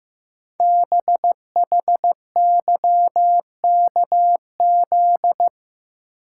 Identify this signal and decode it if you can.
Morse code, heard as a keyed tone, about 15 wpm, 700 Hz. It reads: BHYKZ
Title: Code Groups - 5 Letters - 15wpm